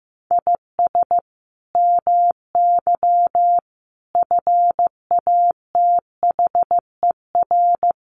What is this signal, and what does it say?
Morse code, heard as a keyed tone, about 15 wpm, 700 Hz. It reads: IS MY FATHER
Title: Sentences from Top 300 Words 15wpm